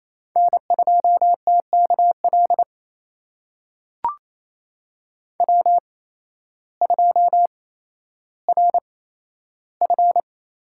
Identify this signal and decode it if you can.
Morse code, heard as a keyed tone, about 28 wpm, spaced as for 13 wpm, 700 Hz. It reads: N2TKL W2RF
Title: USA Call Signs - 5 Characters - 8x Character Spacing - 28wpm